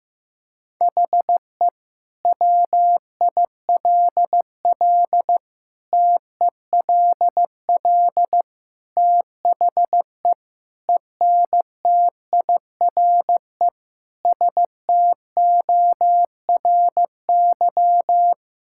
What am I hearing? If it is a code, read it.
Morse code, heard as a keyed tone, about 15 wpm, 700 Hz. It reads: HE WILL TELL THE ENTIRE STORY
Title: Sentences from Top 1000 Words - Review Part 5 - 15wpm